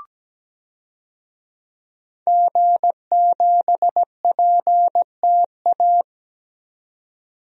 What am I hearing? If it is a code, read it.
Morse code, heard as a keyed tone, about 17 wpm, 700 Hz. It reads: G7PTA
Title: DXCC-100 Call Signs - 5 Characters - 17wpm